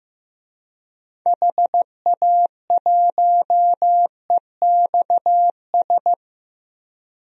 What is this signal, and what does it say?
Morse code, heard as a keyed tone, about 16 wpm, 700 Hz. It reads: HA1EXS
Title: DXCC-100 Call Signs - 6 Characters - Repeated 3x - 15wpm